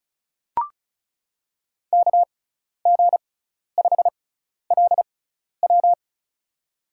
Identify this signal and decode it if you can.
Morse code, heard as a keyed tone, about 36 wpm, spaced as for 12 wpm, 700 Hz. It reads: KG5LW
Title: USA Call Signs - 6x Character Spacing - 35wpm